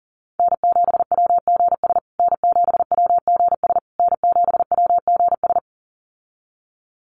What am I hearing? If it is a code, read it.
Morse code, heard as a keyed tone, about 40 wpm, 700 Hz. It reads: N7WGS N7WGS N7WGS
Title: USA Call Signs Repeated 3x 40wpm